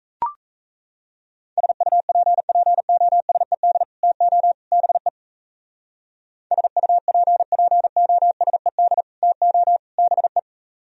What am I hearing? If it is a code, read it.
Morse code, heard as a keyed tone, about 32 wpm, 700 Hz. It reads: SUPPOSED TO BE SUPPOSED TO BE
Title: Speed-Racing - Sets of 3 Words 28wpm